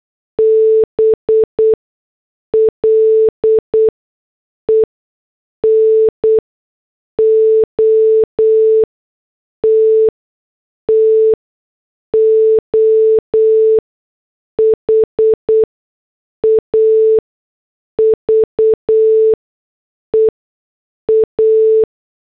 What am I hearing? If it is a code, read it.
Morse code, heard as a keyed tone, about 8 wpm, spaced as for 6 wpm, 440 Hz. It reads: BLE NOT TO HAVE A